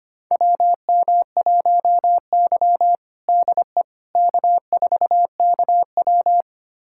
Morse code, heard as a keyed tone, about 25 wpm, 700 Hz. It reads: WM1Y DE K4KW
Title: USA Callsign DE Callsign - 4 Characters - 25wpm